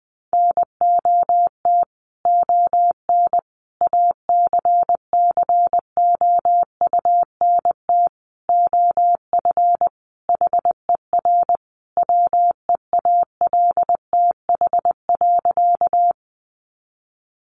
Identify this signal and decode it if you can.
Morse code, heard as a keyed tone, about 20 wpm, 700 Hz. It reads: NOT ON ACCOUNT OF HER WEALTH.
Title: Sentences from Top 1200 Words 20wpm